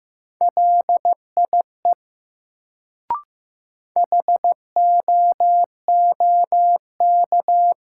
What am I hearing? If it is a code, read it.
Morse code, heard as a keyed tone, about 15 wpm, 700 Hz. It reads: LIE HOOK
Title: Sentences from Top 1600 Words - New Word Drill Alt - 15wpm